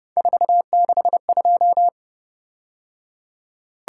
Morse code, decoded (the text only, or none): 462